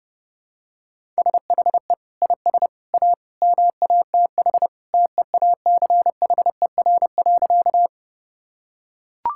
Morse code, heard as a keyed tone, about 30 wpm, 700 Hz. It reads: SHE IS A MATH TEACHER.
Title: Sentences from Top 1000 Words - Review Part 6 - 30wpm